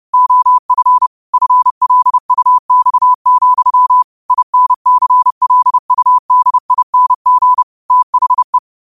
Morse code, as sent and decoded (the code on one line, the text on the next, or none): --- ..-. / ..-. .-.. ..- -..- --..-- / .. -. -.-. .-.. ..- -.. .. -. --. / - .... .
OF FLUX, INCLUDING THE